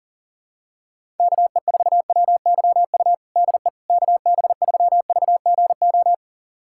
Morse code, heard as a keyed tone, about 40 wpm, 700 Hz. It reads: KE4WYU DE KB3VGO
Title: USA Callsign DE Callsign - 6 Characters - 40wpm